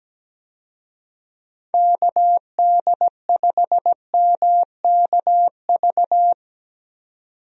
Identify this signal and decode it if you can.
Morse code, heard as a keyed tone, about 17 wpm, 700 Hz. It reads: KD5MKV
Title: USA Call Signs - 6 Characters - 17wpm